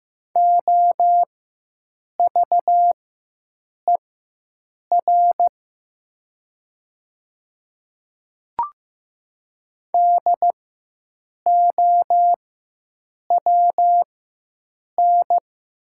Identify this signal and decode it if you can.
Morse code, heard as a keyed tone, about 15 wpm, spaced as for 7 wpm, 700 Hz. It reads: OVER DOWN
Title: 4x Character Spacing - Top 300 Words - 15wpm